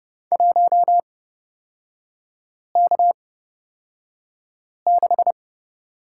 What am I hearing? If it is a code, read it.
Morse code, heard as a keyed tone, about 30 wpm, spaced as for 5 wpm, 700 Hz. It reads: 1K6
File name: Lesson 37e - Letter x - Three Characters 30wpm